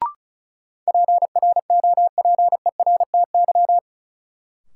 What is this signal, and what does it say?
Morse code, heard as a keyed tone, about 36 wpm, 700 Hz. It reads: PROPERTY
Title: Sentences from Top 1000 Words - New Word Drill - 35wpm